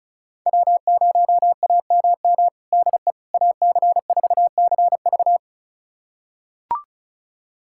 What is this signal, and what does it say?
Morse code, heard as a keyed tone, about 34 wpm, 700 Hz. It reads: W0AMM DE AC4CV
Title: USA Callsign DE Callsign 35wpm